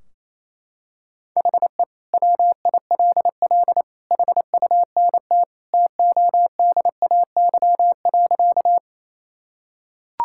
Morse code, heard as a keyed tone, about 28 wpm, 700 Hz. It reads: HE WILL HUNT TODAY.
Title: Sentences from Top 1600 Words 28wpm